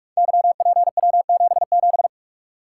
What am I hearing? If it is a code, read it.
Morse code, heard as a keyed tone, about 44 wpm, 700 Hz. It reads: YPW77
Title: Code Groups - 5 Letter:Numbers - 45wpm